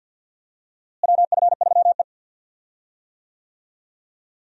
Morse code, heard as a keyed tone, about 46 wpm, 700 Hz. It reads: WF3E
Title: USA Call Signs - 4 Characters - Repeated 2x - 50wpm